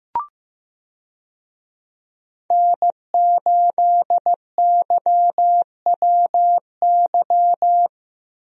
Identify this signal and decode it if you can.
Morse code, heard as a keyed tone, about 15 wpm, 700 Hz. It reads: N8YWY
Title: USA Call Signs - 5 Characters - Repeated 3x - 15wpm